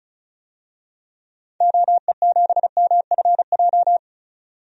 Morse code, heard as a keyed tone, about 34 wpm, 700 Hz. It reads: OE7MFJ
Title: DXCC-100 Call Signs - 6 Characters - 35wpm